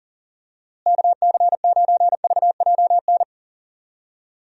Morse code, heard as a keyed tone, about 40 wpm, 700 Hz. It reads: KC9VJN
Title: USA Call Signs - 6 Characters - Repeated 2x - 40wpm